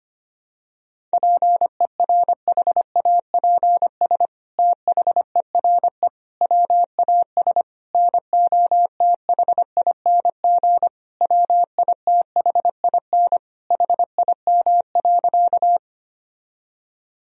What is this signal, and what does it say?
Morse code, heard as a keyed tone, about 25 wpm, 700 Hz. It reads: PERHAPS THERE WAS NOTHING WITHIN HIM.
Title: Sentences from Top 1000 Words - Review Part 4 - 25wpm